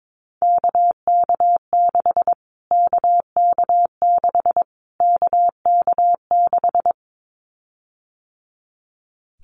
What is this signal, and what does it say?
Morse code, heard as a keyed tone, about 22 wpm, 700 Hz. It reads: KK6 KK6 KK6
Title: Callsign Prefixes for Top 25 DXCC - Repeated 3x - 22wpm